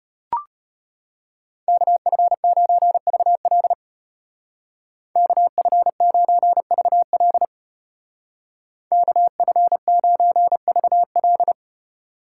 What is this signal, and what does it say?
Morse code, heard as a keyed tone, about 32 wpm, 700 Hz. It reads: KF9VL KF9VL KF9VL